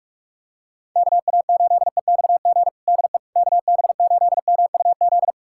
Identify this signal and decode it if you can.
Morse code, heard as a keyed tone, about 42 wpm, 700 Hz. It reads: KA8EXG DE KB8MUZ
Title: USA Callsign DE Callsign 45wpm